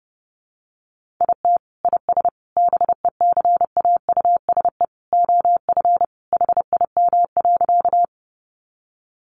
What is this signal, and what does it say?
Morse code, heard as a keyed tone, about 30 wpm, 700 Hz. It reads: IT IS BECAUSE OF HIM.
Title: Sentences from Top 600 Words 30wpm